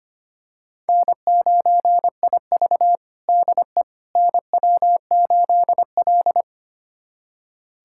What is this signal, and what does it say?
Morse code, heard as a keyed tone, about 25 wpm, 700 Hz. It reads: N9IV DE NW8L
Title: USA Callsign DE Callsign - 4 Characters - 25wpm